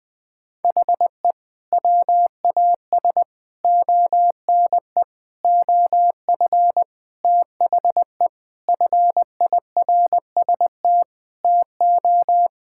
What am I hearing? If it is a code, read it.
Morse code, heard as a keyed tone, about 20 wpm, 700 Hz. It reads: HE WAS ONE OF THE FIRST TO
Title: Sentences with 3-Word Phrases - 20wpm